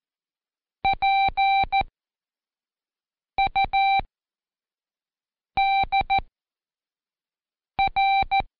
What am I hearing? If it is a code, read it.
Morse code, heard as a keyed tone, about 14 wpm, spaced as for 5 wpm, 780 Hz. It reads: PUDR
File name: Code Quick / AudioDownld1 / Seg26-Day3-Review UDPR